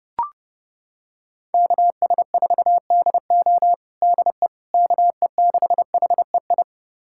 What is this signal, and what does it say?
Morse code, heard as a keyed tone, about 30 wpm, 700 Hz. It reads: KS4DO DE KE6HEI
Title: USA Callsign DE Callsign 30wpm